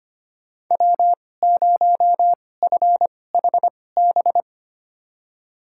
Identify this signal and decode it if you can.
Morse code, heard as a keyed tone, about 25 wpm, spaced as for 18 wpm, 700 Hz. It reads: W0FHB
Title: USA Call Signs - 2x Character Spacing - 25wpm